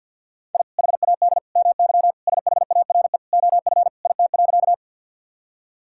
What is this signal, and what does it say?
Morse code, heard as a keyed tone, about 48 wpm, 700 Hz. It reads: I HAD MY SHARE OF IT.
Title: Sentences from Top 1000 Words - Review Part 6 - 50wpm